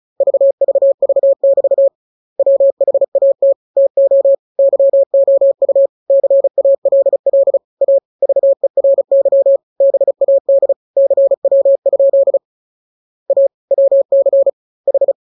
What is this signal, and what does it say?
Morse code, heard as a keyed tone, about 34 wpm, 550 Hz. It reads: VVV<BT> WHAT TO YOU CALL A VERY BAD CW? A WC H
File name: jokes / joke 8 35 0000